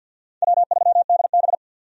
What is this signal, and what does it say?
Morse code, heard as a keyed tone, about 52 wpm, 700 Hz. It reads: W3DB
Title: USA Call Signs - 4 Characters - 50wpm